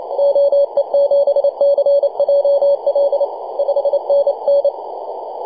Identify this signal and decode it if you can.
Morse code, heard as a keyed tone, about 28 wpm, 560 Hz, with radio noise. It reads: OE7CJL 5NN
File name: posts / 2015-dec-05-0655z-e51xgi-10-115mhz-cw